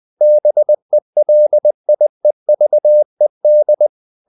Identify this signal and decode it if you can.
Morse code, heard as a keyed tone, about 20 wpm, 600 Hz. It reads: BELIEVED
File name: the old man and the sea part 2 20wpm